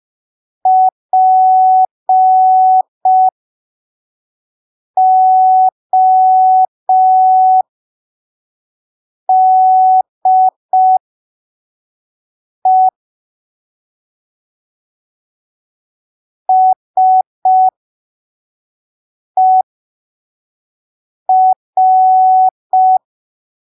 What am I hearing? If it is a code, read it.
Morse code, heard as a keyed tone, about 5 wpm, spaced as for 3 wpm, 750 Hz. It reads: PODE SER
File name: arquivos / texto / texto 18 05ppm